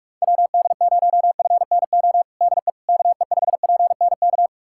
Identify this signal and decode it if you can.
Morse code, heard as a keyed tone, about 46 wpm, 700 Hz. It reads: WD0FNO DE KE5PNK
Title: USA Callsign DE Callsign - 6 Characters - 45wpm